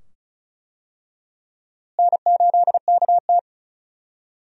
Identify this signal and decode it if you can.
Morse code, heard as a keyed tone, about 36 wpm, 700 Hz. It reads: N8KT
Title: USA Call Signs - 4 Characters - Repeated 2x - 35wpm